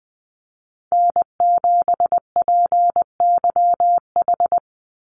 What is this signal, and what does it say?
Morse code, heard as a keyed tone, about 19 wpm, 700 Hz. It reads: N7PYH
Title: USA Call Signs 20wpm